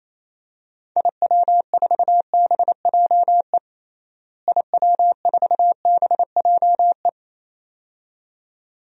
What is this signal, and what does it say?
Morse code, heard as a keyed tone, about 28 wpm, 700 Hz. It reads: IW4BJE IW4BJE
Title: DXCC-100 Call Signs - 6 Characters - Repeated 2x - 28wpm